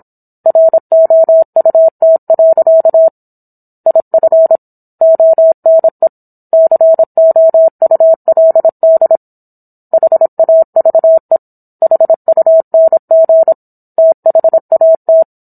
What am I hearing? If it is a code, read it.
Morse code, heard as a keyed tone, about 27 wpm, 650 Hz. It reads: ROUT. IF ONE COULD HAVE HUNG THAT